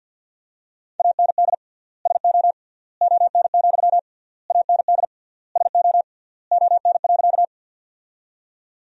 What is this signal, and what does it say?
Morse code, heard as a keyed tone, about 52 wpm, 700 Hz. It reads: AND SO ON, AND SO ON.
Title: 3x Word-Spacing - Sentences from Top 100 Words - 50wpm